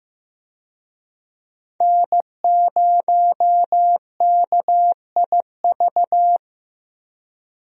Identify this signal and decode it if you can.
Morse code, heard as a keyed tone, about 15 wpm, 700 Hz. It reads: N0KIV
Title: USA Call Signs - 5 Characters - 15wpm